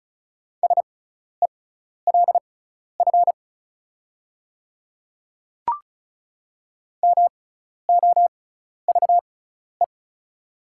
Morse code, heard as a keyed tone, about 36 wpm, spaced as for 12 wpm, 700 Hz. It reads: SELF MOVE
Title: 6x Character Spacing - Top 200 Words - 35wpm